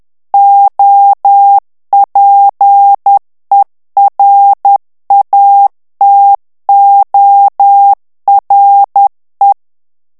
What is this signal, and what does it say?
Morse code, heard as a keyed tone, about 10 wpm, 800 Hz. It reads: OPERATORE
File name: ITALIEN texte clair 2 vitesse 10 mots minute